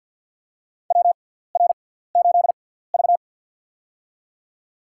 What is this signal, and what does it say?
Morse code, heard as a keyed tone, about 54 wpm, spaced as for 17 wpm, 700 Hz. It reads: WR8V
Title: USA Call Signs - 4 Characters - 6x Character Spacing - 50wpm